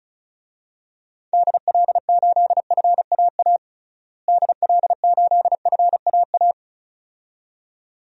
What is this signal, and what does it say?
Morse code, heard as a keyed tone, about 36 wpm, 700 Hz. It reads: DL8FAA DL8FAA
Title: DXCC-100 Call Signs - 6 Characters - Repeated 2x - 35wpm